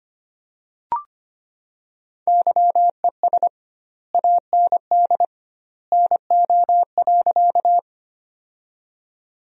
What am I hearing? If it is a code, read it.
Morse code, heard as a keyed tone, about 25 wpm, 700 Hz. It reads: YES AND NO.
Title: 2x Word-Spacing - Sentences from Top 500 Words - 25wpm